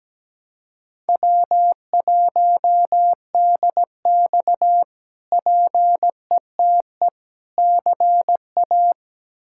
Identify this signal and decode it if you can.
Morse code, heard as a keyed tone, about 17 wpm, 700 Hz. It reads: W1DX PETE CA
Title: K1USN SST QSO Practice - 17wpm